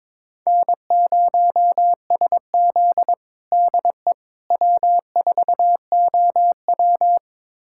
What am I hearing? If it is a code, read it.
Morse code, heard as a keyed tone, about 22 wpm, 700 Hz. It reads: N0SZ DE W4OW
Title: USA Callsign DE Callsign - 4 Characters - 22wpm